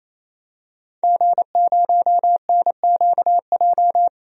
Morse code, heard as a keyed tone, about 28 wpm, 700 Hz. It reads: G0NQJ
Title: DXCC-100 Call Signs - 5 Characters - 28wpm